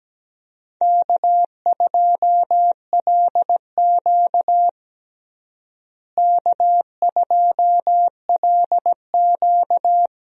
Morse code, heard as a keyed tone, about 17 wpm, 700 Hz. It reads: K2LQ K2LQ
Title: USA Call Signs - 4 Characters - Repeated 3x - 17wpm